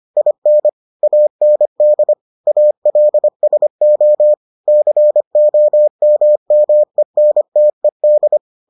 Morse code, heard as a keyed tone, about 25 wpm, 600 Hz. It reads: IN AND ALSO COMMENTED